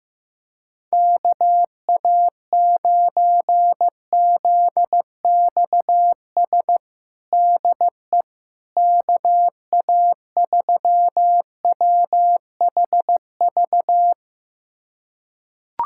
Morse code, heard as a keyed tone, about 16 wpm, 700 Hz. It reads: KA9ZXS DE KA3WHV
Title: USA Callsign DE Callsign - 6 Characters - 15wpm